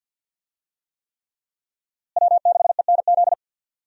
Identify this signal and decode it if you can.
Morse code, heard as a keyed tone, about 52 wpm, 690 Hz. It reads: W6ENZ
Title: USA Call Signs - 5 Characters - Repeated 3x - 50wpm